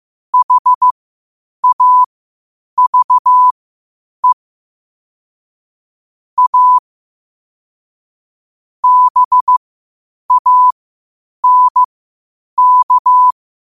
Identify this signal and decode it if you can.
Morse code, heard as a keyed tone, about 15 wpm, spaced as for 8 wpm, 1000 Hz. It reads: HAVE A BANK
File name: qotdfiles / 8wpm